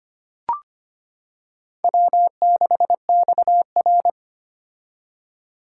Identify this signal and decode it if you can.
Morse code, heard as a keyed tone, about 25 wpm, 700 Hz. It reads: W6XR
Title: K1USN SST QSO Practice - 25wpm